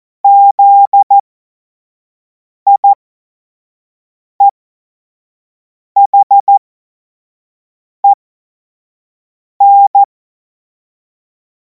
Morse code, heard as a keyed tone, about 14 wpm, spaced as for 5 wpm, 800 Hz. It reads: ZIEHEN
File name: Morsekurs / Witze / Witz 12. Mai 2026 25 F70